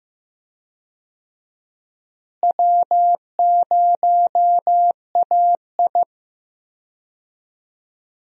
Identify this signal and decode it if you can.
Morse code, heard as a keyed tone, about 15 wpm, 700 Hz. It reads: W0AI
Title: USA Call Signs - 4 Characters - Repeated 2x - 15wpm